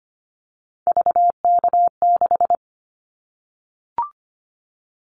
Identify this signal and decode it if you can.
Morse code, heard as a keyed tone, about 25 wpm, 700 Hz. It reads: VK6
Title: Callsign Prefixes for Top 50 DXCC - 25wpm